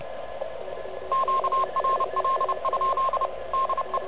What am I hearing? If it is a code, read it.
Morse code, heard as a keyed tone, about 30 wpm, 1060 Hz, with radio noise. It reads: QRL? DE
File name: truSDX